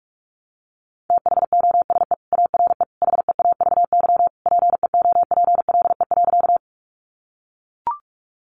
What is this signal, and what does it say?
Morse code, heard as a keyed tone, about 44 wpm, 700 Hz. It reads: THOSE ARE HEAVY PEOPLE.
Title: Sentences from Top 1100 Words 45wpm